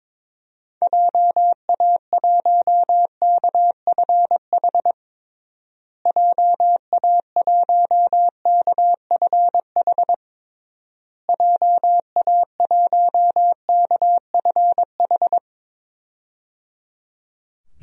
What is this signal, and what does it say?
Morse code, heard as a keyed tone, about 22 wpm, 700 Hz. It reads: JA1KFH JA1KFH JA1KFH
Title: DXCC-100 Call Signs - 6 Characters - Repeated 3x - 22wpm